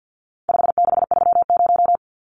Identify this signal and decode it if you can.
Morse code, heard as a keyed tone, about 50 wpm, 700 Hz. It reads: 5630
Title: Numbers - 4-Digit 50wpm